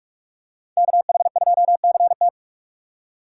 Morse code, heard as a keyed tone, about 41 wpm, 700 Hz. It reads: KH2CT
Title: USA Call Signs - 5 Characters - 4x Character Spacing - 45wpm